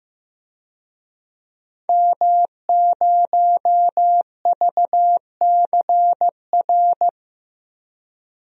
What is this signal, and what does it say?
Morse code, heard as a keyed tone, about 15 wpm, 700 Hz. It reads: M0VCR
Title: DXCC-100 Call Signs - 5 Characters - 15wpm